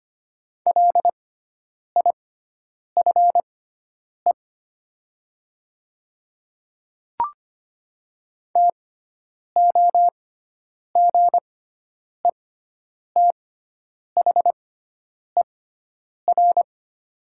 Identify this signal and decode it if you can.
Morse code, heard as a keyed tone, about 25 wpm, spaced as for 9 wpm, 700 Hz. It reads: LIFE TOGETHER
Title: 6x Character Spacing - Top 300 Words - 25wpm